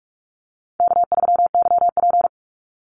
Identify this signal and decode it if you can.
Morse code, heard as a keyed tone, about 44 wpm, 700 Hz. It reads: K3YP
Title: Call Signs Repeated 2x - 45wpm